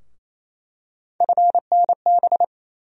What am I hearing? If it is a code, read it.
Morse code, heard as a keyed tone, about 28 wpm, 700 Hz. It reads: FNB
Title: USA Call Sign Postfixes - 3 Letters 28wpm